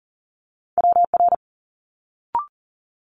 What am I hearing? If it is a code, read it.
Morse code, heard as a keyed tone, about 38 wpm, 700 Hz. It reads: WR
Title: Multi-letter Phonemes Quick Fire 40wpm